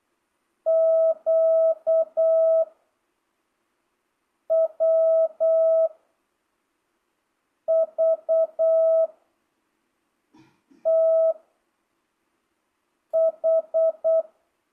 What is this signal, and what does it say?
Morse code, heard as a keyed tone, about 8 wpm, 660 Hz. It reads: Q W V T H
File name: jh8cbh / 2025cw